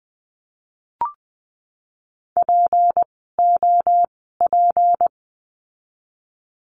Morse code, heard as a keyed tone, about 20 wpm, spaced as for 15 wpm, 700 Hz. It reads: POP
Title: 153 Lesson 25f - Letter p - Words 20wpm